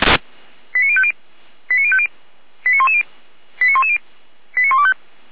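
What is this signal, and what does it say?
Five-tone selective call (ZVEI), heard as quick runs of five tones, tones of about 70 ms. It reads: 89050 89050 89110 89110 89125
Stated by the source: ZVEI five-tone call received on SDR.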